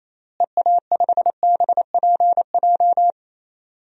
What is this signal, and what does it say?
Morse code, heard as a keyed tone, about 28 wpm, 700 Hz. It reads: EA5BPJ
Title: DXCC-100 Call Signs - 6 Characters - Repeated 3x - 28wpm